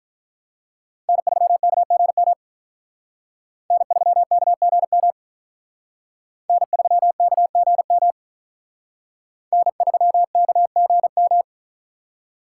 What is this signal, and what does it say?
Morse code, heard as a keyed tone, about 44 wpm, 700 Hz. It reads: N3KGM N3KGM N3KGM N3KGM